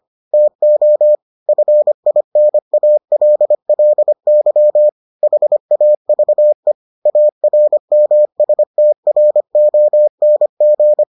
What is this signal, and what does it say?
Morse code, heard as a keyed tone, about 25 wpm, 600 Hz. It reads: TO FINALLY HAVE ARMSTRONG